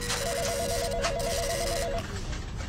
There is no code.